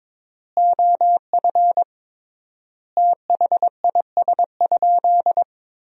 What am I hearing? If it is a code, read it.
Morse code, heard as a keyed tone, about 22 wpm, 700 Hz. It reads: OF THIS?
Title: 3x Word-Spacing - Sentences from Top 200 Words - 22wpm